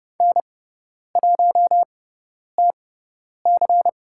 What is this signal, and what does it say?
Morse code, heard as a keyed tone, about 30 wpm, spaced as for 10 wpm, 700 Hz. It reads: N1TC
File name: Lesson 16g - Letter c - Call Signs 30wpm